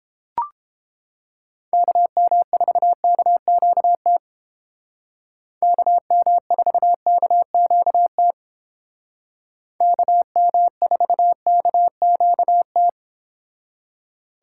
Morse code, heard as a keyed tone, about 29 wpm, 700 Hz. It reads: KM4KQT KM4KQT KM4KQT